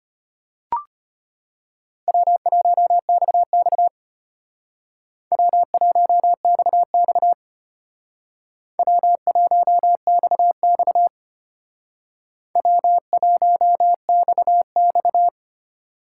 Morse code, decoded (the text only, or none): W1XX W1XX W1XX W1XX